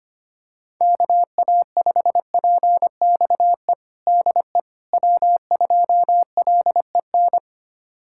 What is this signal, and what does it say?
Morse code, heard as a keyed tone, about 25 wpm, 700 Hz. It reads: KA5PXE DE W2LEN
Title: USA Callsign DE Callsign 25wpm